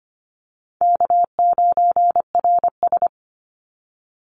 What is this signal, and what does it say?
Morse code, heard as a keyed tone, about 25 wpm, 700 Hz. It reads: K9RS
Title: Easy Call Signs Repeated 2x - 25wpm